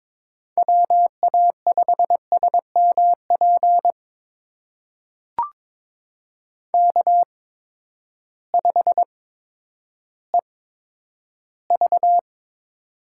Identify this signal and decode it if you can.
Morse code, heard as a keyed tone, about 22 wpm, spaced as for 11 wpm, 700 Hz. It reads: WA5SMP K5EV
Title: USA Call Signs - 8x Character Spacing - 22wpm